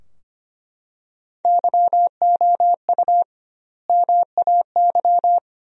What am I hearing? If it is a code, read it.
Morse code, heard as a keyed tone, about 25 wpm, 700 Hz. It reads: YOU MAY
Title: 2x Word-Spacing - Sentences from Top 400 Words - 25wpm